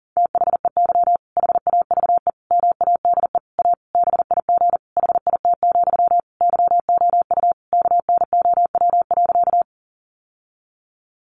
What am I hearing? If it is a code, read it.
Morse code, heard as a keyed tone, about 40 wpm, 700 Hz. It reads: THEY HAVE MADE A BIG HIT, YOU KNOW.
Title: Sentences from Top 1200 Words 40wpm